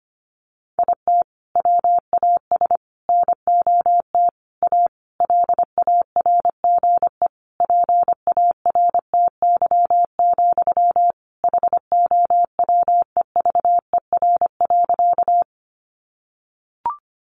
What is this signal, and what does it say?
Morse code, heard as a keyed tone, about 25 wpm, 700 Hz. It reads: IT WAS NOT A LARGE PARTY, HOWEVER.
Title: Sentences from Top 1400 Words 25wpm